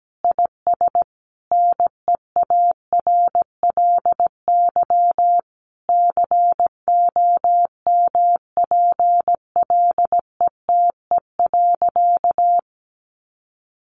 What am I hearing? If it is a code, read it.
Morse code, heard as a keyed tone, about 17 wpm, 700 Hz. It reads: IS NEARLY COMPLETE.
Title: Sentences from Top 1300 Words 17wpm